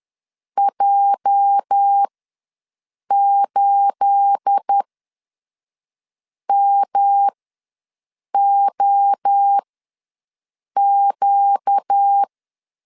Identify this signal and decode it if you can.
Morse code, heard as a keyed tone, about 10 wpm, spaced as for 5 wpm, 800 Hz. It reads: J8MOQ
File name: LEZIONI / 9 28 ws 10wpm 4124